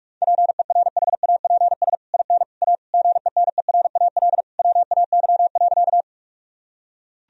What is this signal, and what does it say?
Morse code, heard as a keyed tone, about 41 wpm, 700 Hz. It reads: PERHAPS IN A GENERAL WAY.
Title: Sentences from Top 1000 Words - Review Part 3 - 45wpm